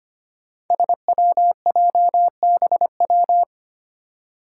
Morse code, heard as a keyed tone, about 25 wpm, 700 Hz. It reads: SWJBW
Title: Five Challenging Letter-Number Combinations 25wpm